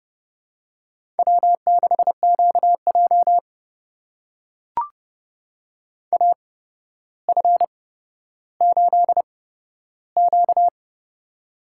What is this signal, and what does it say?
Morse code, heard as a keyed tone, about 30 wpm, spaced as for 13 wpm, 700 Hz. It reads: W6QJ AF8Q
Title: USA Call Signs - 4 Characters - 8x Character Spacing - 30wpm